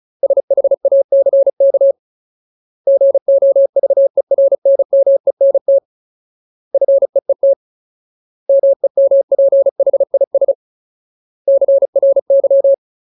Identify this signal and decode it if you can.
Morse code, heard as a keyed tone, about 34 wpm, 550 Hz. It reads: SHACK GOVERNMENT FEET MEMPHIS CRY
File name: Ultimate Word List / 35WPM Ultimate